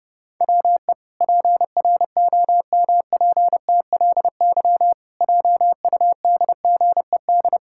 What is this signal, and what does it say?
Morse code, heard as a keyed tone, about 30 wpm, 700 Hz. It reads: WE PROMPTLY JUDGED
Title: Pangram Challenge v2 - 30wpm